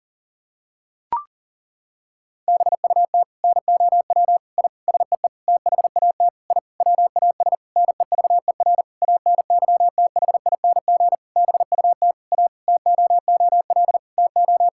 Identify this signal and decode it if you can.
Morse code, heard as a keyed tone, about 40 wpm, 700 Hz. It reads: BUT NOW I SEE THAT I WAS NEVER ANYTHING BUT A TOOL TO
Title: Sentences from Top 1000 Words - Review Part 6 - 40wpm